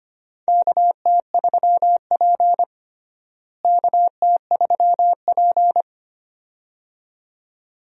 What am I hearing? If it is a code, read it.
Morse code, heard as a keyed tone, about 25 wpm, 700 Hz. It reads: KT3P KT3P
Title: USA Call Signs - 4 Characters - Repeated 2x - 25wpm